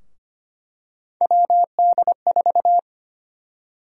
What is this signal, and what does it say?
Morse code, heard as a keyed tone, about 25 wpm, 700 Hz. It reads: WD4
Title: USA Call Sign Prefixes - Repeated 2x - 25wpm